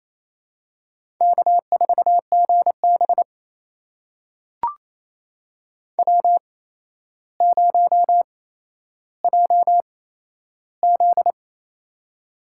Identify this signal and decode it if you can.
Morse code, heard as a keyed tone, about 28 wpm, spaced as for 12 wpm, 700 Hz. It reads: K4GB W0JZ
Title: USA Call Signs - 4 Characters - 8x Character Spacing - 28wpm